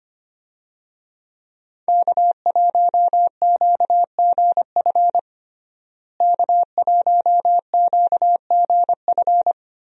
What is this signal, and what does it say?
Morse code, heard as a keyed tone, about 25 wpm, 700 Hz. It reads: K1QGF K1QGF
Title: USA Call Signs - 5 Characters - Repeated 3x - 25wpm